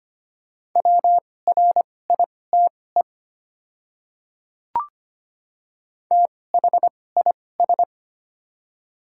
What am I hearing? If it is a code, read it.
Morse code, heard as a keyed tone, about 25 wpm, spaced as for 18 wpm, 690 Hz. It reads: WRITE THIS
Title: 2x Character Spacing - Top 100 Words - 25wpm